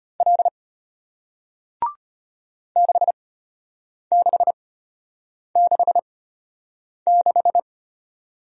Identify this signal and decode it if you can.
Morse code, heard as a keyed tone, about 33 wpm, spaced as for 8 wpm, 700 Hz. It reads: L 6666